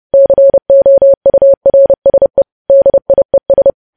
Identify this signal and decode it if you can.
Morse code, heard as a keyed tone, about 30 wpm, 550 Hz. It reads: COURSE DIES
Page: news headlines in morse code at 30 wpm